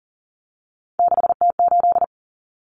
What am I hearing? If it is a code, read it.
Morse code, heard as a keyed tone, about 40 wpm, 700 Hz. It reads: 6T8
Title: Common Cut Numbers - 3-Digit 40wpm